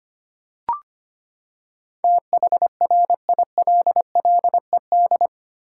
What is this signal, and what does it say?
Morse code, heard as a keyed tone, about 25 wpm, 700 Hz. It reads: THRILLED
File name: Eight Letter Words - 25wpm